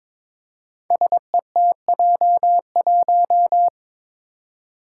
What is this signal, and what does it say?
Morse code, heard as a keyed tone, about 22 wpm, 700 Hz. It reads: SETJ1
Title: Five Challenging Letter-Number Combinations 22wpm